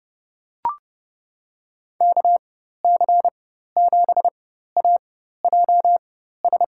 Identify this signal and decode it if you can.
Morse code, heard as a keyed tone, about 30 wpm, spaced as for 14 wpm, 700 Hz. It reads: KC7AJS
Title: USA Call Signs - 6 Characters - 4x Character Spacing - 30wpm